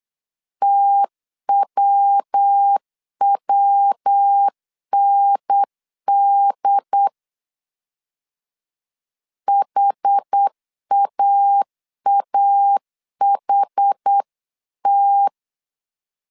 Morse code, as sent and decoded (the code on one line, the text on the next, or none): - .-- .-- -. -.. / .... .- .- .... -
TWWND HAAHT